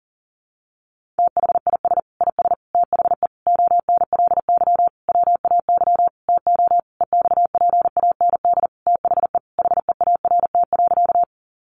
Morse code, heard as a keyed tone, about 40 wpm, 700 Hz. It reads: THIS IS THE ONLY WAY TO EXPAND THE HEART.
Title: Sentences from Top 1400 Words 40wpm